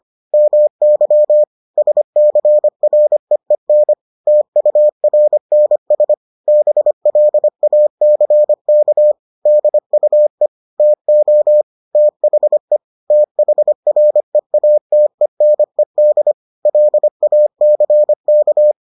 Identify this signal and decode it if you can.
Morse code, heard as a keyed tone, about 25 wpm, 600 Hz. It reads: MY SCREEN TURNS BLACK DUE TO THE THREATENED LACK